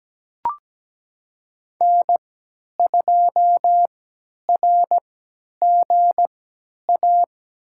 Morse code, heard as a keyed tone, about 17 wpm, spaced as for 10 wpm, 700 Hz. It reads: N2RGA
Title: USA Call Signs - 5 Characters - 3x Character Spacing - 17wpm